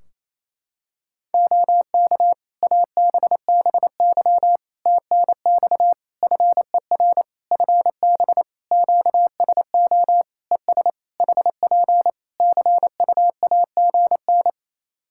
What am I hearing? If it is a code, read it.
Morse code, heard as a keyed tone, about 27 wpm, 700 Hz. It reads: OK ABBY TNX FER FB QSO ES HP CUAGN
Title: Practice QSO - Exchange 4 - Ending - 28wpm